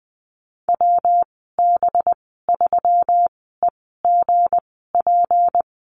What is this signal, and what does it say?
Morse code, heard as a keyed tone, about 20 wpm, spaced as for 15 wpm, 700 Hz. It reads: WB3EGP
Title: 231 Lesson 37g - Letter x - Call Signs 20wpm